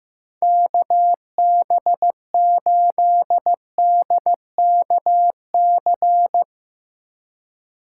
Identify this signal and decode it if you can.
Morse code, heard as a keyed tone, about 16 wpm, 700 Hz. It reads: KB8DKC
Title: Speed Racing - USA 2x3 Call Signs 15wpm